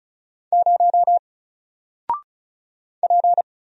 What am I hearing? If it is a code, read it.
Morse code, heard as a keyed tone, about 36 wpm, 700 Hz. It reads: 0 P
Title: Single Letter-Number 35wpm